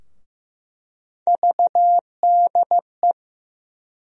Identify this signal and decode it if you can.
Morse code, heard as a keyed tone, about 15 wpm, 700 Hz. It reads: VDE
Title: USA Call Sign Postfixes - 3 Letters 15wpm